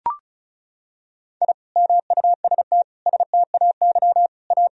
Morse code, heard as a keyed tone, about 36 wpm, 700 Hz. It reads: I MUST STAY A
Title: Sentences from Top 1000 Words - Review Part 1 - 35wpm